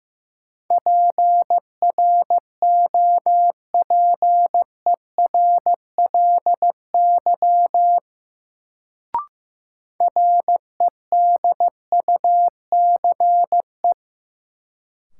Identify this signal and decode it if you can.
Morse code, heard as a keyed tone, about 15 wpm, 700 Hz. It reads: PROPERLY REDUCE
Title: Sentences from Top 1100 Words - New Word Drill Alt - 15wpm